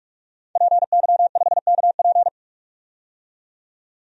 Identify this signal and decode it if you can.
Morse code, heard as a keyed tone, about 46 wpm, 700 Hz. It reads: PY5KP
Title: DXCC-100 Call Signs - 5 Characters - 45wpm